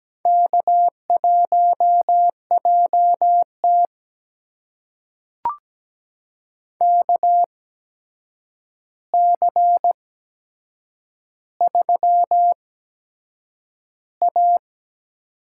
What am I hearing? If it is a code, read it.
Morse code, heard as a keyed tone, about 17 wpm, spaced as for 7 wpm, 700 Hz. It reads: K1JT KC3A
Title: USA Call Signs - 4 Characters - 8x Character Spacing - 17wpm